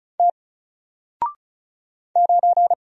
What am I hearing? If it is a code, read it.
Morse code, heard as a keyed tone, about 33 wpm, 700 Hz. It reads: T 9
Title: Single Letter-Number Alt 35wpm